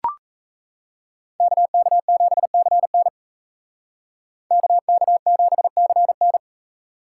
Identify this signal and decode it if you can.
Morse code, heard as a keyed tone, about 36 wpm, 700 Hz. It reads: KK7CN KK7CN